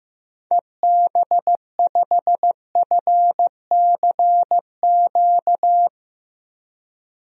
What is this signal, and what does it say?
Morse code, heard as a keyed tone, about 15 wpm, 700 Hz. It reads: EB5FCQ
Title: DXCC-100 Call Signs - 6 Characters - Repeated 2x - 15wpm